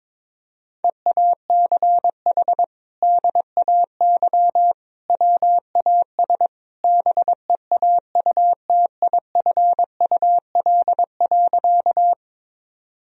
Morse code, heard as a keyed tone, about 21 wpm, 700 Hz. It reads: EACH DAY WAS BEAUTIFUL.
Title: Sentences from Top 1000 Words - Review Part 1 - 22wpm